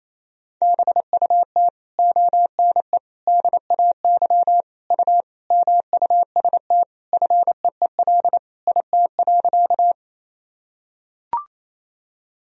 Morse code, heard as a keyed tone, about 28 wpm, 700 Hz. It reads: BUT ONE DAY U MUST FEEL IT.
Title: Sentences from Top 400 Words and Abbreviations 28wpm